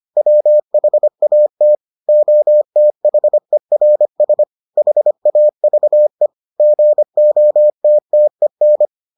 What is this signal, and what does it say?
Morse code, heard as a keyed tone, about 25 wpm, 600 Hz. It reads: WHAT OTHERS HAVE GOTTEN